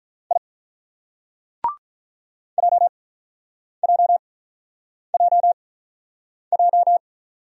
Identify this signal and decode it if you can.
Morse code, heard as a keyed tone, about 48 wpm, 700 Hz. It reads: I J J J J